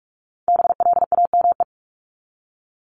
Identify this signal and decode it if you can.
Morse code, heard as a keyed tone, about 44 wpm, 700 Hz. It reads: BLAME
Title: Top 1751-2000 Words 45wpm